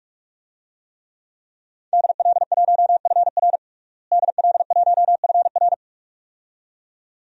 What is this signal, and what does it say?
Morse code, heard as a keyed tone, about 42 wpm, 700 Hz. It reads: DL1FR DL1FR
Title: DXCC-100 Call Signs - 5 Characters - Repeated 2x - 45wpm